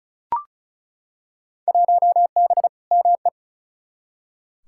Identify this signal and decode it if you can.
Morse code, heard as a keyed tone, about 34 wpm, 700 Hz. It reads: 1B ME
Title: Field Day Exchanges - 35wpm